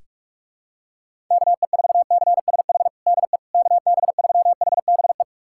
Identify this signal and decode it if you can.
Morse code, heard as a keyed tone, about 46 wpm, 700 Hz. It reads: KE4CSH DE KB3HBE
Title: USA Callsign DE Callsign 45wpm